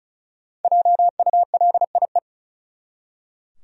Morse code, heard as a keyed tone, about 36 wpm, 700 Hz. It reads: JULIE
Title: USA Common Names - Max 5 Letters - 35wpm